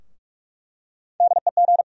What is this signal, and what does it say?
Morse code, heard as a keyed tone, about 46 wpm, 700 Hz. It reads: DEG
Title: USA Call Sign Postfixes - 3 Letters 45wpm